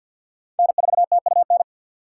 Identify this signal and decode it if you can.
Morse code, heard as a keyed tone, about 52 wpm, 700 Hz. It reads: N4TUN
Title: USA Call Signs - 6x Character Spacing - 50wpm